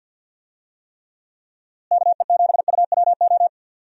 Morse code, heard as a keyed tone, about 50 wpm, 700 Hz. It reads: KE7UWO
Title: USA Call Signs - 6 Characters - Repeated 3x - 50wpm